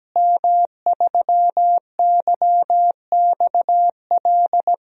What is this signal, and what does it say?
Morse code, heard as a keyed tone, about 17 wpm, 700 Hz. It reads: M3YXL
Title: DXCC-100 Call Signs - 5 Characters - Repeated 2x - 17wpm